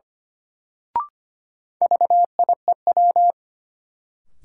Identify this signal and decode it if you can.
Morse code, heard as a keyed tone, about 25 wpm, 700 Hz. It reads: VIEW
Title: Sentences from Top 900 Words - New Word Drill - 25wpm